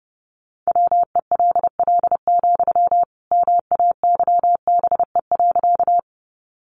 Morse code, heard as a keyed tone, about 30 wpm, 700 Hz. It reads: WELL, MAYBE.
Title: Sentences from Top 1000 Words 30wpm